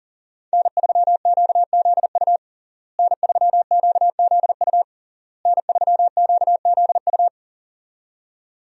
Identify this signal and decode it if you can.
Morse code, heard as a keyed tone, about 37 wpm, 700 Hz. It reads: N3QZU N3QZU N3QZU
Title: USA Call Signs - 5 Characters - Repeated 3x - 40wpm